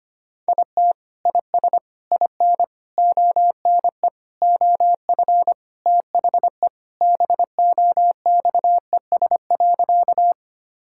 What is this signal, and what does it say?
Morse code, heard as a keyed tone, about 25 wpm, 700 Hz. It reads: IT IS IN ONE OF THE BOXES.
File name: practice / 25wpm / Sentences with 3-Word Phrases - 25wpm